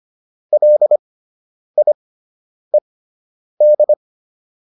Morse code, heard as a keyed tone, about 25 wpm, spaced as for 9 wpm, 600 Hz. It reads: LIED